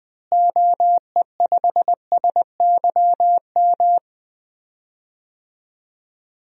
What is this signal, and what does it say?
Morse code, heard as a keyed tone, about 21 wpm, 700 Hz. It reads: OE5SYM
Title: DXCC-100 Call Signs - 6 Characters - Repeated 2x - 20wpm